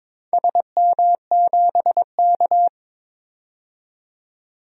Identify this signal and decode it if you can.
Morse code, heard as a keyed tone, about 22 wpm, 700 Hz. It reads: SM7K
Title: DXCC-100 Call Signs - 4 Characters - 22wpm